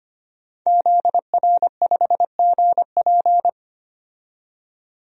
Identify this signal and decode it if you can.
Morse code, heard as a keyed tone, about 25 wpm, 700 Hz. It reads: ZR5GP
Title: DXCC-100 Call Signs - 5 Characters - 25wpm